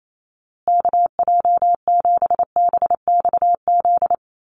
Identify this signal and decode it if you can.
Morse code, heard as a keyed tone, about 28 wpm, 700 Hz. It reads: KJ7BXZ
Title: USA Call Signs Repeated 3x 28wpm